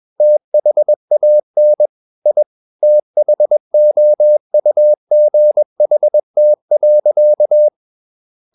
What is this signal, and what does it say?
Morse code, heard as a keyed tone, about 21 wpm, 600 Hz. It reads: THAN I THOUGHT.
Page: CW med Gnister